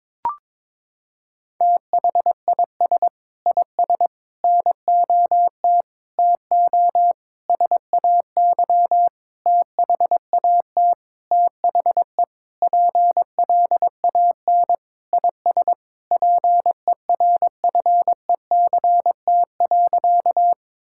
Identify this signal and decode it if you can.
Morse code, heard as a keyed tone, about 22 wpm, 700 Hz. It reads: THIS IS NOT TO SAY THAT THE PLAN IS PERFECT.
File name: Sentences with 7-Word Phrases - 22wpm